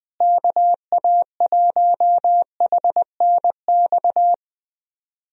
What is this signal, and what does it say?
Morse code, heard as a keyed tone, about 20 wpm, 700 Hz. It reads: KA1HNX
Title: USA Call Signs - 6 Characters - Repeated 2x - 20wpm